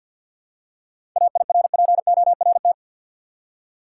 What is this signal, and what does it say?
Morse code, heard as a keyed tone, about 52 wpm, 700 Hz. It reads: AIRPORT
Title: Seven Letter Words - 50wpm